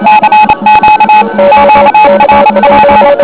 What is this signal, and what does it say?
Morse code, heard as a keyed tone, about 27 wpm, 850 Hz, with noise. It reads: CQ QRP
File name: OK QRP 2007 / REC31